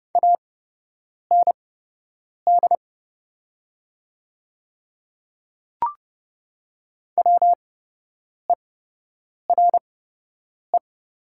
Morse code, heard as a keyed tone, about 30 wpm, spaced as for 8 wpm, 700 Hz. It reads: AND WERE
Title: 8x Character Spacing - Top 100 Words - 30wpm